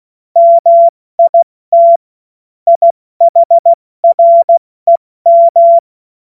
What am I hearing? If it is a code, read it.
Morse code, heard as a keyed tone, about 16 wpm, 680 Hz. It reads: MIT IHREM
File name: Morsekurs / Witze / Witz 03. Mai 2022 70 F80